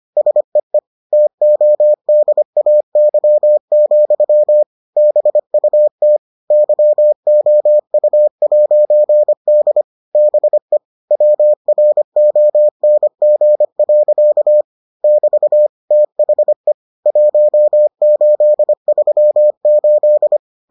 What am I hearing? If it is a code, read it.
Morse code, heard as a keyed tone, about 25 wpm, 600 Hz. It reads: SEE TODAY, BUT YOU'D BE WRONG. <BT> THE 1838